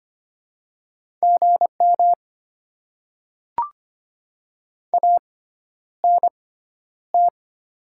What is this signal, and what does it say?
Morse code, heard as a keyed tone, about 25 wpm, spaced as for 11 wpm, 700 Hz. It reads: GM ANT
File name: Top 20 QSO Elements - 6x Character Spacing - 25wpm